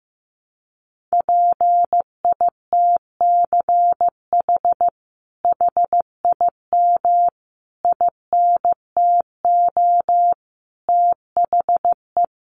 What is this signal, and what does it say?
Morse code, heard as a keyed tone, about 15 wpm, 700 Hz. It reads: PITCH HIM INTO THE
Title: Sentences from Top 1300 Words 15wpm